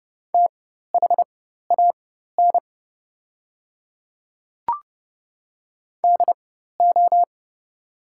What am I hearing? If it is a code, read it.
Morse code, heard as a keyed tone, about 31 wpm, spaced as for 14 wpm, 700 Hz. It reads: THAN DO
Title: 4x Character Spacing - Top 100 Words - 30wpm